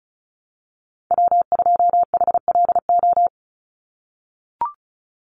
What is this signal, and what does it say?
Morse code, heard as a keyed tone, about 34 wpm, 700 Hz. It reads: W2HLO
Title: USA Call Signs 35wpm